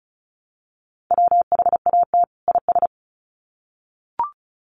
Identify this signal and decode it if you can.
Morse code, heard as a keyed tone, about 34 wpm, 700 Hz. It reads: WHAT IS
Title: Sets of 2 Words 35wpm